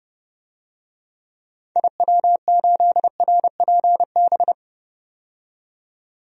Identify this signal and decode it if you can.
Morse code, heard as a keyed tone, about 30 wpm, 700 Hz. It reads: IW8RPB
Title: DXCC-100 Call Signs - 6 Characters - 30wpm